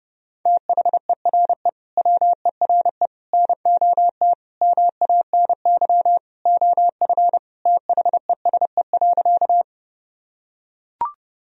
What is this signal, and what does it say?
Morse code, heard as a keyed tone, about 30 wpm, 700 Hz. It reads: THERE WERE NOT MANY OF THESE.
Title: Sentences from Top 1000 Words - Review Part 1 - 30wpm